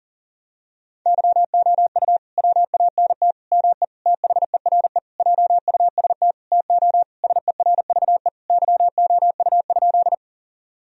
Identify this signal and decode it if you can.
Morse code, heard as a keyed tone, about 37 wpm, 700 Hz. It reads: YOU WANT ME THERE JUST TO SERVE YOU?
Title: Sentences from Top 1000 Words - Review Part 2 - 40wpm